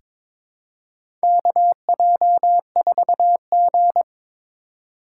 Morse code, heard as a keyed tone, about 22 wpm, 700 Hz. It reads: KJ4G
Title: USA Call Signs - 4 Characters - 4x Character Spacing - 22wpm